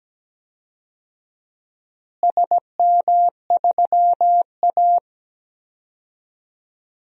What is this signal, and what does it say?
Morse code, heard as a keyed tone, about 17 wpm, 700 Hz. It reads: SM3A
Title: DXCC-100 Call Signs - 4 Characters - Repeated 3x - 17wpm